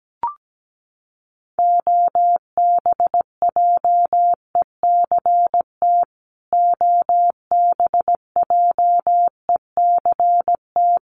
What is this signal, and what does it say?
Morse code, heard as a keyed tone, about 17 wpm, 700 Hz. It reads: OBJECT OBJECT
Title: Top 500 Words Repeated 17wpm